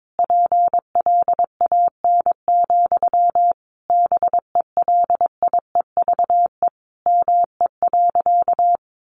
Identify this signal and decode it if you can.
Morse code, heard as a keyed tone, about 22 wpm, 700 Hz. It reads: PLAN, BELIEVE ME.
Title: Sentences from Top 1000 Words 22wpm